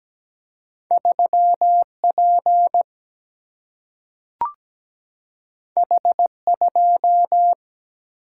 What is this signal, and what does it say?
Morse code, heard as a keyed tone, about 17 wpm, 700 Hz. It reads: 3 P H 2
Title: Single-Letter Single-Number 17wpm